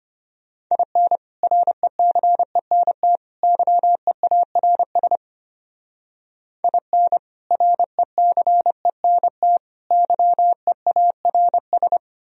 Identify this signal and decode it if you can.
Morse code, heard as a keyed tone, about 30 wpm, 700 Hz. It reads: IN RECENT YEARS IN RECENT YEARS
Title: Speed-Racing - Sets of 3 Words 25wpm